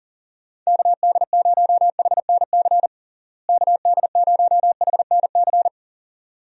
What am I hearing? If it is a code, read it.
Morse code, heard as a keyed tone, about 40 wpm, 700 Hz. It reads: KD0HNC KD0HNC
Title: USA Call Signs - 6 Characters - Repeated 2x - 40wpm